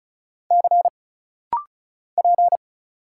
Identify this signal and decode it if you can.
Morse code, heard as a keyed tone, about 35 wpm, 700 Hz. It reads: C P
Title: Single Letter-Number Alt - Rapid-Fire 35wpm